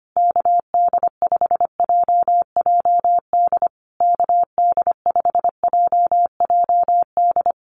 Morse code, heard as a keyed tone, about 25 wpm, 700 Hz. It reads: KD5JJD KD5JJD
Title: USA Call Signs Repeated 3x 25wpm